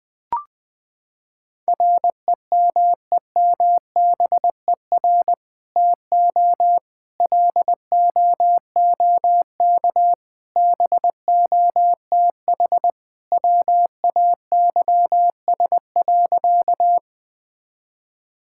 Morse code, heard as a keyed tone, about 21 wpm, 700 Hz. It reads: REMEMBER TO LOOK BOTH WAYS.
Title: Sentences with 2-Word Phrases - 20wpm